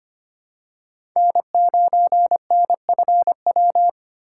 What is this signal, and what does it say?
Morse code, heard as a keyed tone, about 25 wpm, 700 Hz. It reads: N9NFW
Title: USA Call Signs - 5 Characters - 3x Character Spacing - 25wpm